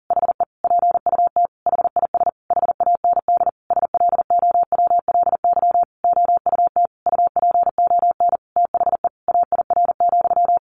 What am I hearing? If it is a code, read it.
Morse code, heard as a keyed tone, about 37 wpm, 700 Hz. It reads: HE PUT HIS HAND SLOWLY OUT UPON THE AIR,
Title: A Sound Of Thunder 40wpm